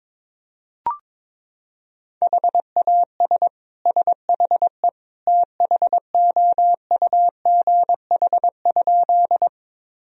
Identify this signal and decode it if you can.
Morse code, heard as a keyed tone, about 21 wpm, 700 Hz. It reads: HAS SHE THOUGH?
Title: Sentences from Top 1000 Words - Review Part 2 - 22wpm